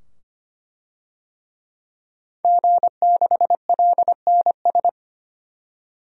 Morse code, heard as a keyed tone, about 25 wpm, 690 Hz. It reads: G6LNS
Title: DXCC-100 Call Signs - 5 Characters - 25wpm